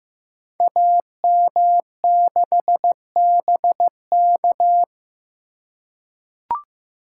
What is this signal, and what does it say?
Morse code, heard as a keyed tone, about 15 wpm, 700 Hz. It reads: AM6BK
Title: Code Groups - 5 Letter:Numbers - 15wpm